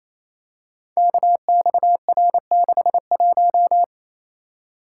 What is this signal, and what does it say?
Morse code, heard as a keyed tone, about 27 wpm, 700 Hz. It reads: KXR61
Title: Code Groups - 5 Letter:Numbers - 28wpm